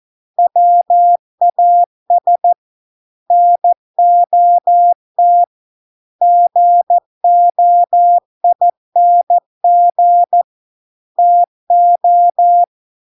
Morse code, heard as a keyed tone, about 14 wpm, 700 Hz. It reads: WAS NOT GOING TO